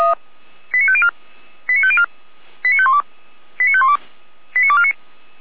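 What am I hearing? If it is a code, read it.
Five-tone selective call (ZVEI), heard as quick runs of five tones, tones of about 70 ms. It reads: 89504 89504 89512 89512 89249
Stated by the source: ZVEI five-tone call 89337 received by SDR, Feuerwehr Orlamünde.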